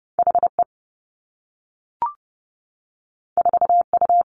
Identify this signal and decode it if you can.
Morse code, heard as a keyed tone, about 30 wpm, 700 Hz. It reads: H E 4 U
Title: Two Challenging Letter-Number Combinations 30wpm